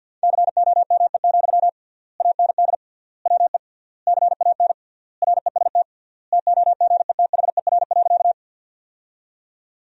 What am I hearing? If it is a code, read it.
Morse code, heard as a keyed tone, about 50 wpm, 700 Hz. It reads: COME, AND WE CAN REST TOGETHER.
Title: 3x Word-Spacing - Sentences from Top 500 Words - 50wpm